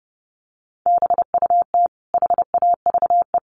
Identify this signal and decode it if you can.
Morse code, heard as a keyed tone, about 30 wpm, 700 Hz. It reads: BUT HAVE
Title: Sentences from Top 1500 Words 30wpm